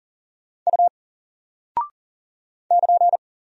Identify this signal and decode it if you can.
Morse code, heard as a keyed tone, about 39 wpm, 700 Hz. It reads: U <KN>
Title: Single Character (Letter:Number:Punctuation:Prosign) 40wpm